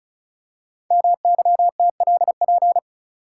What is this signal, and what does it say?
Morse code, heard as a keyed tone, about 36 wpm, 700 Hz. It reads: MYTLP
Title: Code Groups - 5 Letters - 35wpm